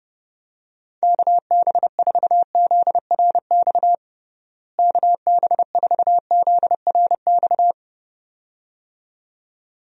Morse code, heard as a keyed tone, about 30 wpm, 700 Hz. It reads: KB4ZRX KB4ZRX
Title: USA Call Signs - 6 Characters - Repeated 2x - 30wpm